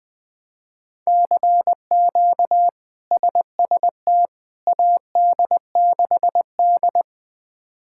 Morse code, heard as a keyed tone, about 20 wpm, 700 Hz. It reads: CQ SST AD6D
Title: K1USN SST QSO Practice - 20wpm